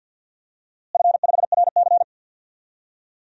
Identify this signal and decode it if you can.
Morse code, heard as a keyed tone, about 52 wpm, 700 Hz. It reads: F5RC
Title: DXCC-100 Call Signs - 4 Characters - Repeated 2x - 50wpm